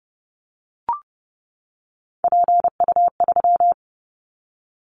Morse code, heard as a keyed tone, about 31 wpm, 700 Hz. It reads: PU3
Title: Callsign Prefixes for Top 100 DXCC - 30wpm